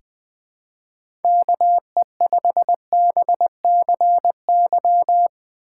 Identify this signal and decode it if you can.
Morse code, heard as a keyed tone, about 20 wpm, 700 Hz. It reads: KE5BCY
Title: K1USN SST QSO Practice - 20wpm